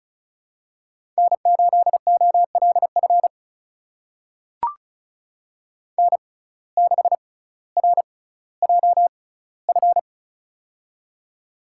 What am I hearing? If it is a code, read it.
Morse code, heard as a keyed tone, about 36 wpm, spaced as for 18 wpm, 700 Hz. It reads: N8OLF N6RJF
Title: USA Call Signs - 5 Characters - 6x Character Spacing - 35wpm